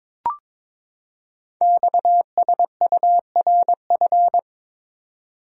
Morse code, heard as a keyed tone, about 23 wpm, 700 Hz. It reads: XSURF
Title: Code Groups - 5 Letters - 22wpm